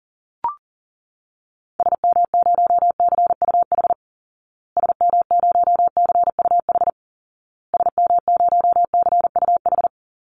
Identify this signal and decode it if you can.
Morse code, heard as a keyed tone, about 37 wpm, 700 Hz. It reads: SM0CUH SM0CUH SM0CUH
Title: Call Signs Repeated 2x - 40wpm